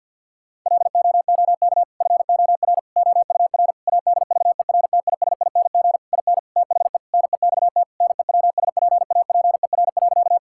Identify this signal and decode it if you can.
Morse code, heard as a keyed tone, about 50 wpm, 700 Hz. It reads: LOOK FOR OUR ADVERTISING IN THE NEXT NEWSPAPER.
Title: Sentences from Top 1000 Words - Review Part 6 - 50wpm